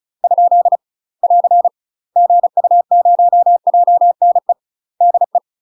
Morse code, heard as a keyed tone, about 36 wpm, 700 Hz. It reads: ? <AR> GU0JNE DE
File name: UK MASTER QSOs TRAINING / 35WPM / UK MASTER QSOs 0036